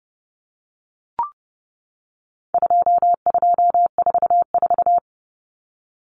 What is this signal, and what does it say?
Morse code, heard as a keyed tone, about 30 wpm, 700 Hz. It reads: 2244
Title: Common Cut Numbers - 4-Digit 30wpm